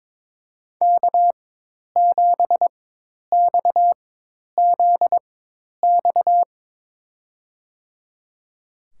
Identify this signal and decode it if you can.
Morse code, heard as a keyed tone, about 22 wpm, spaced as for 10 wpm, 700 Hz. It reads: K7XZX
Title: USA Call Signs - 4x Character Spacing - 22wpm